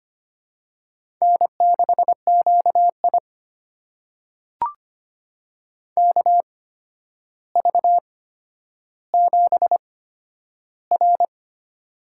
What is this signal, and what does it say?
Morse code, heard as a keyed tone, about 25 wpm, spaced as for 11 wpm, 700 Hz. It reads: N6QI KV7R
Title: USA Call Signs - 4 Characters - 8x Character Spacing - 25wpm